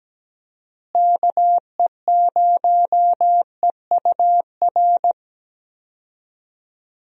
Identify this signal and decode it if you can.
Morse code, heard as a keyed tone, about 17 wpm, 700 Hz. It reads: KE0EUR
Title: USA Call Signs - 6 Characters - Repeated 2x - 17wpm